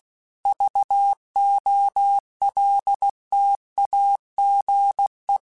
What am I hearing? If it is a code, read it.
Morse code, heard as a keyed tone, about 16 wpm, 800 Hz. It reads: VOLTAGE
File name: Technical Subjects / Technical Subjects-mp3s / XC @15